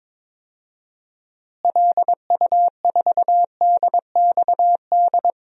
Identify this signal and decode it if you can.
Morse code, heard as a keyed tone, about 22 wpm, 700 Hz. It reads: LU4DXD
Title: DXCC-100 Call Signs - 6 Characters - Repeated 3x - 22wpm